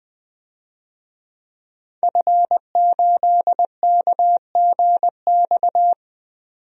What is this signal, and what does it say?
Morse code, heard as a keyed tone, about 21 wpm, 700 Hz. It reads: F8KGX
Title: DXCC-100 Call Signs - 5 Characters - 20wpm